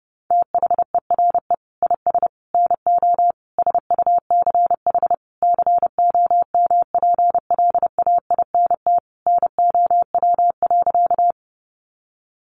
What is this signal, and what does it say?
Morse code, heard as a keyed tone, about 30 wpm, 700 Hz. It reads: THERE IS NO SUCH COMPLAINT NOW.
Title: Sentences from Top 1400 Words 30wpm